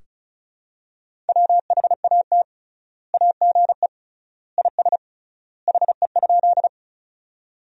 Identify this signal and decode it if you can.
Morse code, heard as a keyed tone, about 34 wpm, 700 Hz. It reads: WHAT AGE IS HE?
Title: 3x Word-Spacing - Sentences from Top 500 Words - 35wpm